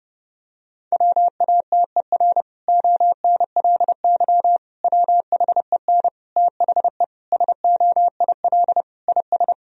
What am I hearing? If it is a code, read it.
Morse code, heard as a keyed tone, about 30 wpm, 700 Hz. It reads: WATER ONLY WHEN THE SOIL IS
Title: Sentences from Top 1000 Words - Review Part 4 - 30wpm